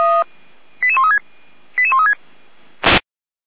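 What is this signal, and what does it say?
Five-tone selective call (ZVEI), heard as quick runs of five tones, tones of about 70 ms. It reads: 88136 88136
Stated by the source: ZVEI five-tone selective call received with an SDR.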